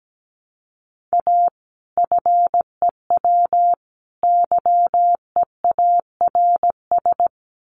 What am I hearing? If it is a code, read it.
Morse code, heard as a keyed tone, about 17 wpm, 700 Hz. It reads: A FEW YEARS
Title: Sets of 3 Words 17wpm